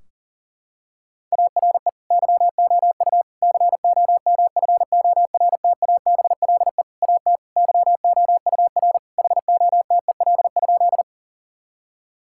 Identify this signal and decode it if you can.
Morse code, heard as a keyed tone, about 40 wpm, 700 Hz. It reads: ARE YOU COMFORTABLE AT YOUR HOTEL?
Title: Sentences from Top 1600 Words 40wpm